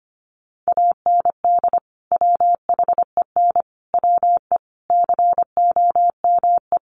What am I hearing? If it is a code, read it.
Morse code, heard as a keyed tone, about 25 wpm, 700 Hz. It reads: AND WHEN WE COME
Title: Sets of 5 Words 25wpm